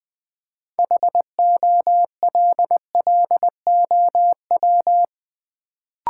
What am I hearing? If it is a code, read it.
Morse code, heard as a keyed tone, about 20 wpm, 690 Hz. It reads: HOLLOW
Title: Build A Word - 3 Letters At A Time - 6-Letter Words - 20wpm